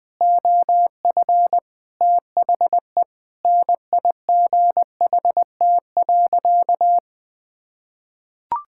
Morse code, heard as a keyed tone, about 20 wpm, 700 Hz. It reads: OF THE NIGHT.
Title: Sentences with 7-Word Phrases - 20wpm